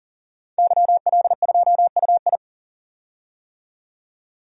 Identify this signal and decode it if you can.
Morse code, heard as a keyed tone, about 40 wpm, 700 Hz. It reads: YL2UI
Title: DXCC-100 Call Signs - 5 Characters - 40wpm